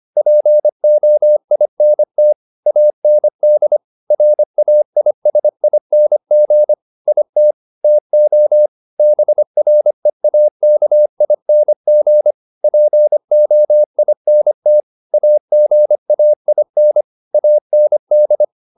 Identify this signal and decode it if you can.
Morse code, heard as a keyed tone, about 25 wpm, 600 Hz. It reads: POINT AND RAISING IT TO BREAKING POINT AGAIN AND